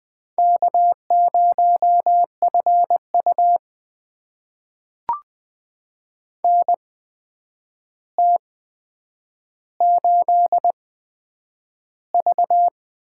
Morse code, heard as a keyed tone, about 20 wpm, spaced as for 9 wpm, 700 Hz. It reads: K0FU NT8V
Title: USA Call Signs - 4 Characters - 8x Character Spacing - 20wpm